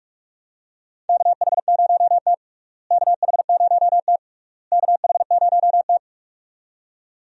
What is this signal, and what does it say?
Morse code, heard as a keyed tone, about 42 wpm, 700 Hz. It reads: KH0T KH0T KH0T
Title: USA Call Signs - 4 Characters - Repeated 3x - 45wpm